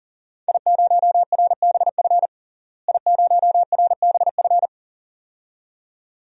Morse code, heard as a keyed tone, about 37 wpm, 700 Hz. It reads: I0RBF I0RBF
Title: DXCC-100 Call Signs - 5 Characters - Repeated 2x - 40wpm